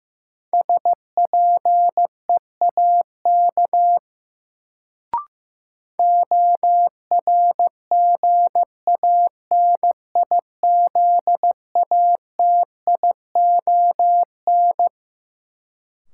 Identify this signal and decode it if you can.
Morse code, heard as a keyed tone, about 15 wpm, 700 Hz. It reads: SPEAK ORGANIZATION
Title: Sentences from Top 1100 Words - New Word Drill Alt - 15wpm